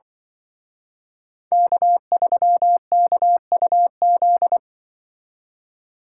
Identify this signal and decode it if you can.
Morse code, heard as a keyed tone, about 24 wpm, 700 Hz. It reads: K3KUZ
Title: Speed Racing - USA 1x3 Call Signs 20wpm